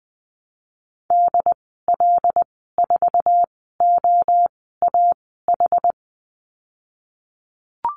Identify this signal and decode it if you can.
Morse code, heard as a keyed tone, about 20 wpm, spaced as for 14 wpm, 700 Hz. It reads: DL4OAH
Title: 076 Lesson 13g - Letter L - Call Signs 20wpm